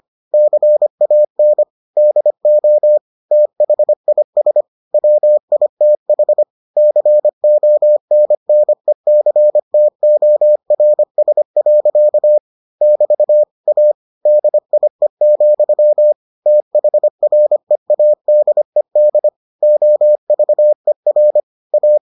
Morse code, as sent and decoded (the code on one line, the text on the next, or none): -.-. .- -. / -.. --- / - .... .. ... / .-- .. - .... / -.-. --- -. -. -.-. - --- .-. ... .-.-.- / -...- / .- / -.. .. --..-- / - .... .-. .- -.. -.. / --- ...- . .-. / .-
CAN DO THIS WITH CONNECTORS. <BT> A DIE, THREADED OVER A